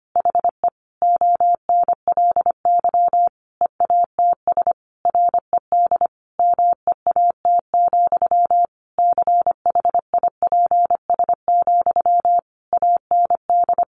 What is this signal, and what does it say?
Morse code, heard as a keyed tone, about 25 wpm, 700 Hz. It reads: HE ONLY EATS RED MEAT, CHIPS, AND
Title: Easy Phrases 25wpm